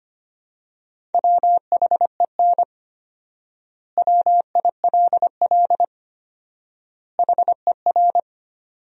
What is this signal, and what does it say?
Morse code, heard as a keyed tone, about 25 wpm, 700 Hz. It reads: WHEN WILL HER
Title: 4x Word-Spacing - Sentences from Top 200 Words - 25wpm